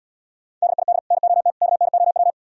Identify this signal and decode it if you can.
Morse code, heard as a keyed tone, about 76 wpm, 700 Hz. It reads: DID NOT KNOW